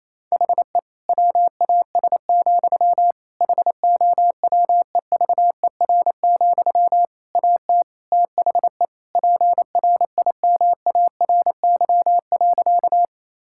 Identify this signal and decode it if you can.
Morse code, heard as a keyed tone, about 28 wpm, 700 Hz. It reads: HE WAS, HOWEVER, AT THE PRIMARY.
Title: Sentences from Top 1000 Words - Review Part 5 - 28wpm